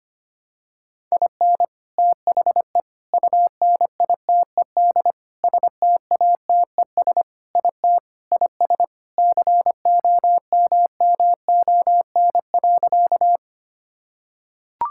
Morse code, heard as a keyed tone, about 25 wpm, 700 Hz. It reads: IN THE UNITED STATES IT IS COMMON.
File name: practice / 25wpm / Sentences with 4-Word Phrases - 25wpm